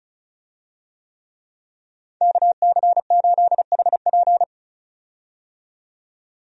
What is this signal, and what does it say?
Morse code, heard as a keyed tone, about 36 wpm, 700 Hz. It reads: KC8HP
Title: USA Call Signs - 5 Characters - 35wpm